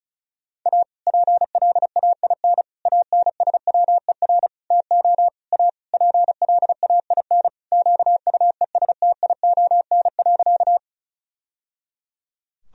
Morse code, heard as a keyed tone, about 34 wpm, 700 Hz. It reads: A PLAIN ANSWER TO A PLAIN QUESTION.
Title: Sentences from Top 1000 Words - Review Part 2 - 35wpm